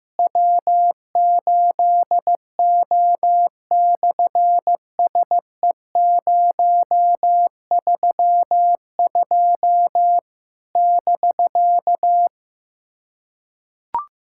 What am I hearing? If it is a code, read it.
Morse code, heard as a keyed tone, about 14 wpm, 700 Hz. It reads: W8O/SE032 <BK>
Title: SOTA QSO Practice 15wpm